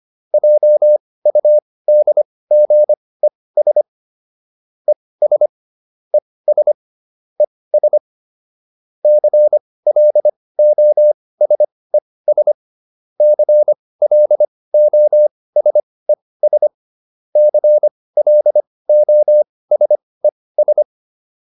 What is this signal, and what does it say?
Morse code, heard as a keyed tone, about 25 wpm, spaced as for 18 wpm, 600 Hz. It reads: JUDGES ES ES ES CLOSES CLOSES CLOSES